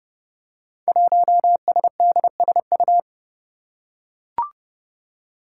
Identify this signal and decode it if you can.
Morse code, heard as a keyed tone, about 30 wpm, 700 Hz. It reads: 1SDSU
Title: Five Challenging Letter-Number Combinations 30wpm